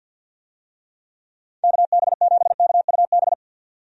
Morse code, heard as a keyed tone, about 50 wpm, 700 Hz. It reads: KB7KUB
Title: USA Call Signs - 6 Characters - 50wpm